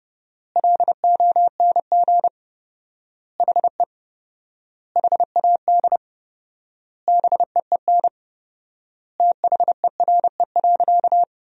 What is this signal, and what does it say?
Morse code, heard as a keyed tone, about 32 wpm, 700 Hz. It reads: LONG HE HAD BEEN THERE.
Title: 4x Word-Spacing - Sentences from Top 300 Words - 30wpm